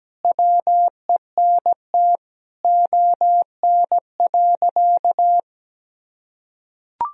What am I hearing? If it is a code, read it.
Morse code, heard as a keyed tone, about 17 wpm, 700 Hz. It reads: WENT ON.
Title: Sentences from Top 1000 Words - Review Part 3 - 17wpm